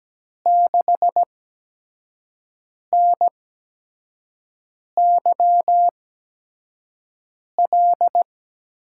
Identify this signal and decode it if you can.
Morse code, heard as a keyed tone, about 17 wpm, spaced as for 5 wpm, 700 Hz. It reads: 6NYL
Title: USA Call Signs - 6 Characters - 8x Character Spacing - 17wpm